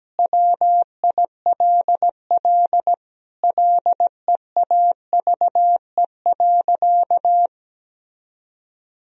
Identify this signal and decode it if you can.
Morse code, heard as a keyed tone, about 17 wpm, 700 Hz. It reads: WILL LEAVE.
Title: Sentences from Top 1000 Words - Review Part 2 - 17wpm